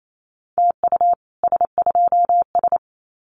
Morse code, heard as a keyed tone, about 28 wpm, 700 Hz. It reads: TU S2S
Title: Common QSO Phrases Encore 28wpm